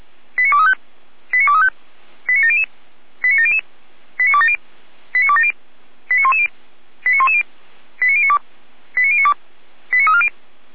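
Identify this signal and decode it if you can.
Five-tone selective call (ZVEI), heard as quick runs of five tones, tones of about 80 ms. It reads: 89235 89235 89700 89700 89270 89270 89110 89110 89092 89092 89340
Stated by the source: ZVEI five-tone selective call received on SDR, from the feuerwehr-orlamuende.org fire brigade audio server.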